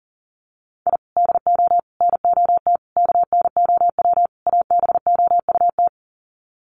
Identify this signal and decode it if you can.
Morse code, heard as a keyed tone, about 40 wpm, 700 Hz. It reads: I DO NOT KNOW ABOUT
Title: Sets of 5 Words 40wpm